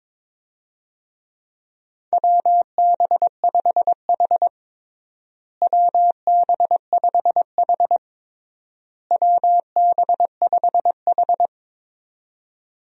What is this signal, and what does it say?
Morse code, heard as a keyed tone, about 22 wpm, 700 Hz. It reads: WB5H WB5H WB5H
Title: USA Call Signs - 4 Characters - Repeated 3x - 22wpm